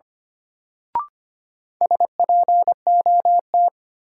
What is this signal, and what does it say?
Morse code, heard as a keyed tone, about 25 wpm, 700 Hz. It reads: SPOT
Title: Sentences from Top 1100 Words - New Word Drill Alt - 25wpm